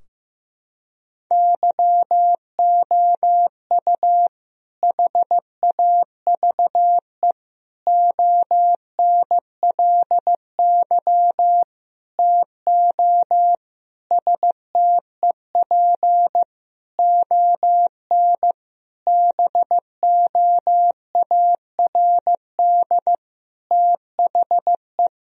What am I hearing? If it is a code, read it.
Morse code, heard as a keyed tone, about 15 wpm, 700 Hz. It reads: YOU HAVE ONLY TO STEP ON BOARD THE
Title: Sentences from Top 1000 Words - Review Part 4 - 15wpm